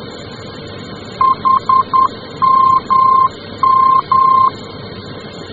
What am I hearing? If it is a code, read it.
Morse code, heard as a keyed tone, about 10 wpm, 1050 Hz. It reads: HMM
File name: germany / mx / HMM-D